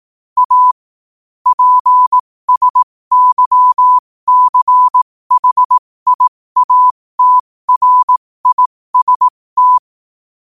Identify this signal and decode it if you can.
Morse code, heard as a keyed tone, about 18 wpm, spaced as for 15 wpm, 1000 Hz. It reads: A PSYCHIATRIST
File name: qotdfiles / 15wpm